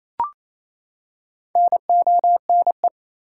Morse code, heard as a keyed tone, about 27 wpm, 700 Hz. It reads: NONE
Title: Four Letter Words - 28wpm